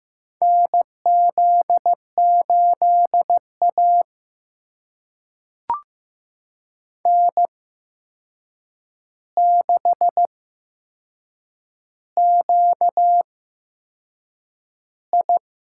Morse code, heard as a keyed tone, about 15 wpm, spaced as for 6 wpm, 700 Hz. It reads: NZ8A N6QI
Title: USA Call Signs - 4 Characters - 8x Character Spacing - 15wpm